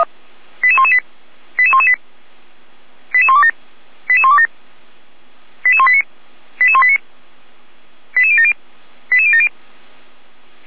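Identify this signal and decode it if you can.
Five-tone selective call (ZVEI), heard as quick runs of five tones, tones of about 70 ms. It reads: 88118 88118 88127 88127 88189 88189 88070 88070
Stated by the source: ZVEI five-tone call 88102 received by SDR, from the Feuerwehr Orlamünde fire brigade audio server.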